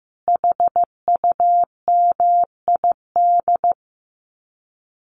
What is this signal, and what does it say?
Morse code, heard as a keyed tone, about 15 wpm, 700 Hz. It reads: HUMID
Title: Common QSO Phrases 15wpm